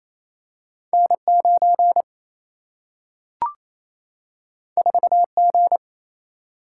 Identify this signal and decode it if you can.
Morse code, heard as a keyed tone, about 28 wpm, 700 Hz. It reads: N9 4G
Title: Single-Letter Single-Number 28wpm